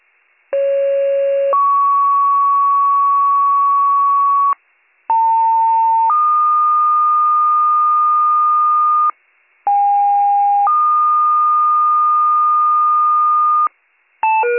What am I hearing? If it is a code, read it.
Two-tone page, heard as a one-second tone, then a three-second tone, about 550 then 1120 Hz, about 900 then 1220 Hz, about 810 then 1220 Hz. It is heard from 0.5 to 13.7 s.